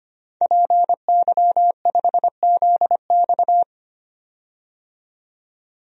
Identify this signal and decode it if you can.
Morse code, heard as a keyed tone, about 25 wpm, 700 Hz. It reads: PY5ZX
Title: DXCC-100 Call Signs - 5 Characters - 25wpm